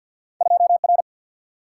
Morse code, heard as a keyed tone, about 51 wpm, 700 Hz. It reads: 2 R
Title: Single-Letter Single-Number 50wpm